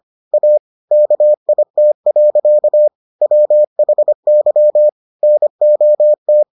Morse code, heard as a keyed tone, about 25 wpm, 600 Hz. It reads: A KIT. WHY NOT